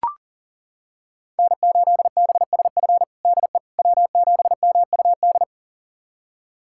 Morse code, heard as a keyed tone, about 37 wpm, 700 Hz. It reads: N8BSF DE W7MUD
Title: USA Callsign DE Callsign - 5 Characters - 40wpm